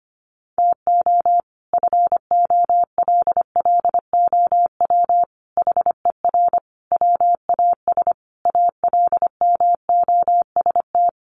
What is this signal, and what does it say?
Morse code, heard as a keyed tone, about 25 wpm, 700 Hz. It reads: TO FOLLOW HER WAS ALMOST
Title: Sentences from Top 1100 Words 25wpm